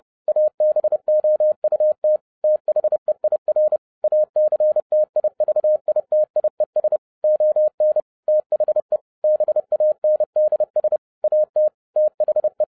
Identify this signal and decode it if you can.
Morse code, heard as a keyed tone, about 30 wpm, 610 Hz. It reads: ABOUT THEIR ACTIVITIES ON THE BANDS AT THE